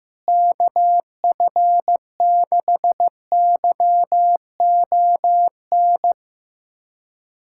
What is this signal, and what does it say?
Morse code, heard as a keyed tone, about 15 wpm, 700 Hz. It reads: KF6YON
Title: USA Callsign DE Callsign 15wpm